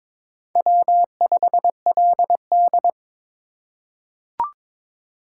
Morse code, heard as a keyed tone, about 22 wpm, 700 Hz. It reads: W5LD
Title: USA Call Signs - 4 Characters - 2x Character Spacing - 22wpm